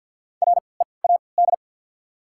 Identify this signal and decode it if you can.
Morse code, heard as a keyed tone, about 51 wpm, spaced as for 28 wpm, 700 Hz. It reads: READ
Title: 3x Character Spacing - Top 200 Words - 50wpm